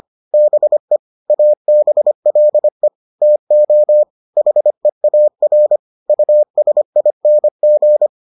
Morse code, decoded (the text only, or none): BE ABLE TO HEAR USING